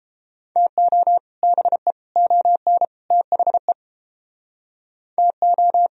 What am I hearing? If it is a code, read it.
Morse code, heard as a keyed tone, about 29 wpm, 700 Hz. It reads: TO BE ON THE TO